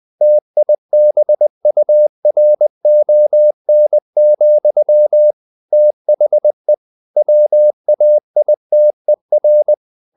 Morse code, heard as a keyed tone, about 20 wpm, 600 Hz. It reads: TIBURON, THE WAITER
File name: the old man and the sea part 2 20wpm